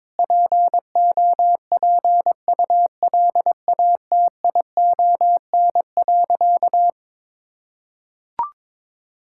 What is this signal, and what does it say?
Morse code, heard as a keyed tone, about 21 wpm, 700 Hz. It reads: POPULATION.
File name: practice / 22wpm / Sentences with 4-Word Phrases - 22wpm